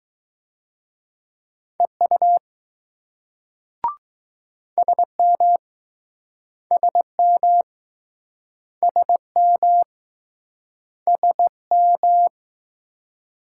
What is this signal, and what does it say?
Morse code, heard as a keyed tone, about 20 wpm, 700 Hz. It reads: EU SM SM SM SM